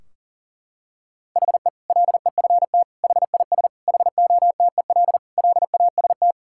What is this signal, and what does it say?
Morse code, heard as a keyed tone, about 40 wpm, 700 Hz. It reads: HE LEFT HIS HOTEL LAST
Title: Sentences from Top 1600 Words 40wpm